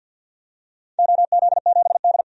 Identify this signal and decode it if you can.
Morse code, heard as a keyed tone, about 48 wpm, 700 Hz. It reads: OZ7D
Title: DXCC-100 Call Signs - 4 Characters - 50wpm